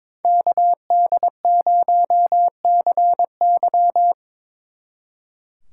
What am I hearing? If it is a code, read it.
Morse code, heard as a keyed tone, about 26 wpm, 700 Hz. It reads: KD0CY
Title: Speed Racing - USA 2x2 Call Signs 22wpm